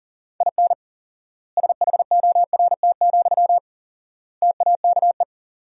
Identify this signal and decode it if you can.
Morse code, heard as a keyed tone, about 40 wpm, 700 Hz. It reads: IN SHORT, TAKE
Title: 4x Word-Spacing - Sentences from Top 400 Words - 40wpm